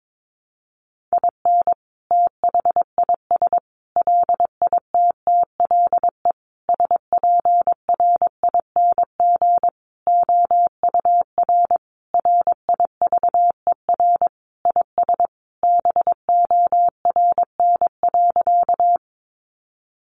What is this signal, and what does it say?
Morse code, heard as a keyed tone, about 22 wpm, 700 Hz. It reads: IN THIS LITTLE SPRING OUR RIVER IS BORN.
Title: Sentences from Top 1200 Words 22wpm